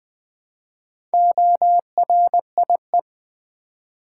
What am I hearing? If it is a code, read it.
Morse code, heard as a keyed tone, about 20 wpm, 700 Hz. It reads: ORIE
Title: Build A Word - 4 Letters At A Time - 11-Letter Words - 20wpm